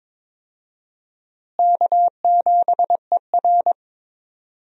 Morse code, heard as a keyed tone, about 22 wpm, 700 Hz. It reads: K7ER
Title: USA Call Signs - 4 Characters - Repeated 2x - 22wpm